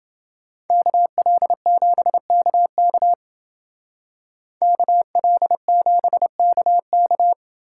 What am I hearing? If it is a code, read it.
Morse code, heard as a keyed tone, about 28 wpm, 700 Hz. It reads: KL7KK KL7KK